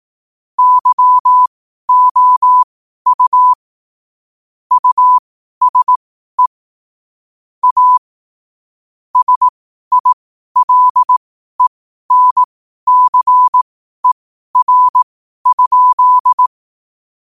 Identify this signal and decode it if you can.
Morse code, heard as a keyed tone, about 18 wpm, spaced as for 12 wpm, 1000 Hz. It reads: YOU USE A SILENCER?
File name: qotdfiles / 12wpm